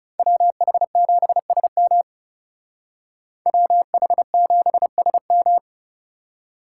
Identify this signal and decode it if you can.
Morse code, heard as a keyed tone, about 38 wpm, 700 Hz. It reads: WH7SM WH7SM